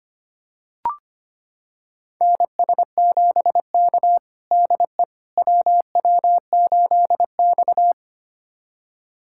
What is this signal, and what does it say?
Morse code, heard as a keyed tone, about 25 wpm, 700 Hz. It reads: NS7K DE WW8X
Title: USA Callsign DE Callsign - 4 Characters - 25wpm